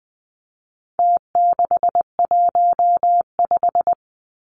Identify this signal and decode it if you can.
Morse code, heard as a keyed tone, about 20 wpm, 700 Hz. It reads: T615
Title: Numbers - Contest Style - 4-Digit 20wpm